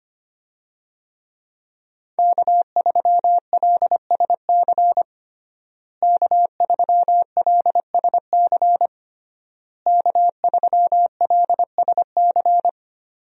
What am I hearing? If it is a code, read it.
Morse code, heard as a keyed tone, about 25 wpm, 700 Hz. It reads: K3LSC K3LSC K3LSC
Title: USA Call Signs - 5 Characters - Repeated 3x - 25wpm